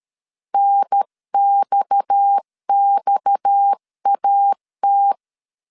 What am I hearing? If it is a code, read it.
Morse code, heard as a keyed tone, about 13 wpm, 800 Hz. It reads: NXXAT
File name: CW / DET / 1 6 1500C S C 12wpm 3533